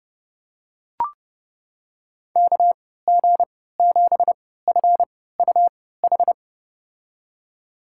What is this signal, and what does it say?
Morse code, heard as a keyed tone, about 30 wpm, spaced as for 17 wpm, 700 Hz. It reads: KG7FUH
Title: USA Call Signs - 6 Characters - 3x Character Spacing - 30wpm